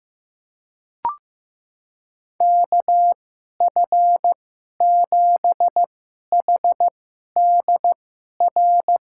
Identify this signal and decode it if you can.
Morse code, heard as a keyed tone, about 15 wpm, spaced as for 11 wpm, 700 Hz. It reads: KF7HDR
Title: USA Call Signs - 2x Character Spacing - 15wpm